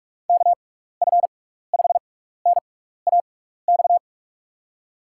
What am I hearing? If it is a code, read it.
Morse code, heard as a keyed tone, about 44 wpm, spaced as for 15 wpm, 700 Hz. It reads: KF5NAX
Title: USA Call Signs - 6 Characters - 6x Character Spacing - 45wpm